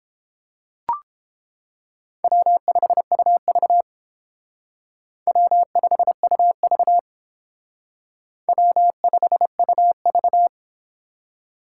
Morse code, heard as a keyed tone, about 29 wpm, 700 Hz. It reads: W5UV W5UV W5UV